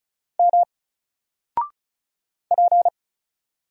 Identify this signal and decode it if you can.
Morse code, heard as a keyed tone, about 36 wpm, 700 Hz. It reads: M P
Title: Single Character (Letter:Number:Punctuation:Prosign) 35wpm